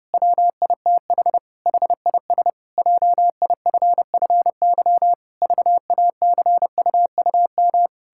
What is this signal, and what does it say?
Morse code, heard as a keyed tone, about 32 wpm, 700 Hz. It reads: WITH HIS JIFFY VACUUM
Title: Pangram Challenge v2 - 30wpm